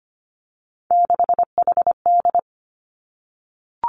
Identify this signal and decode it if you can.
Morse code, heard as a keyed tone, about 25 wpm, 700 Hz. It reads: 6HD
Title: Three Challenging Letter-Number Combinations 25wpm